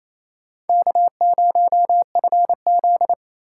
Morse code, heard as a keyed tone, about 28 wpm, 700 Hz. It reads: K0FZ
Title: USA Call Signs - 4x Character Spacing - 28wpm